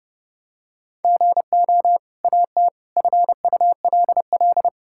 Morse code, heard as a keyed tone, about 30 wpm, 700 Hz. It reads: GO AT FULL
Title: Sentences from Top 1000 Words - Review Part 5 - 30wpm